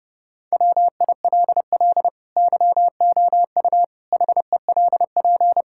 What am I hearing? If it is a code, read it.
Morse code, heard as a keyed tone, about 31 wpm, 700 Hz. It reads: WILL YOU HELP
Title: Sentences from Top 1000 Words - Review Part 6 - 30wpm